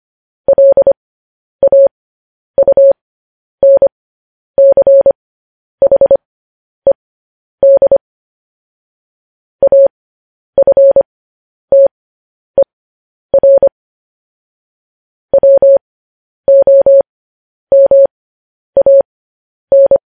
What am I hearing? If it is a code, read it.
Morse code, heard as a keyed tone, about 25 wpm, spaced as for 10 wpm, 550 Hz. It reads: LAUNCHED AFTER WOMAN
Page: News Headlines in Morse Code at 10 WPM